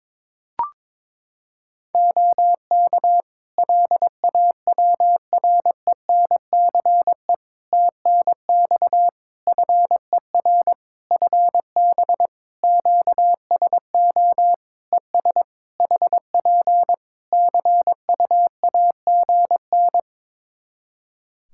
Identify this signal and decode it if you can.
Morse code, heard as a keyed tone, about 22 wpm, 700 Hz. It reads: OK LAWRENCE TNX FER FB QSO ES HP CUAGN
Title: Practice QSO - Exchange 4 - Ending - 22wpm